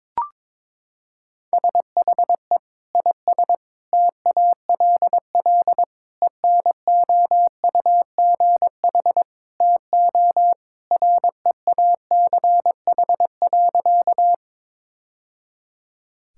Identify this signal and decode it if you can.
Morse code, heard as a keyed tone, about 22 wpm, 700 Hz. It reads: SHE IS TALL ENOUGH TO REACH.
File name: practice / 22wpm / Sentences with 2-Word Phrases - 22wpm